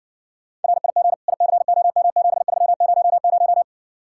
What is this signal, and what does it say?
Morse code, heard as a keyed tone, about 69 wpm, 700 Hz. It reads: RIG ICOM7300